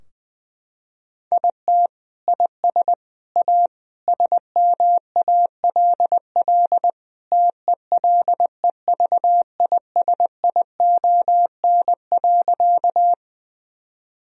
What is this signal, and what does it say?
Morse code, heard as a keyed tone, about 20 wpm, 700 Hz. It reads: IT IS A SMALL TELEVISION.
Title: Sentences from Top 1000 Words - Review Part 4 - 20wpm